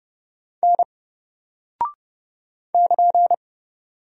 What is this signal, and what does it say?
Morse code, heard as a keyed tone, about 30 wpm, 700 Hz. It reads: N <KN>
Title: Single Character (Letter:Number:Punctuation:Prosign) 30wpm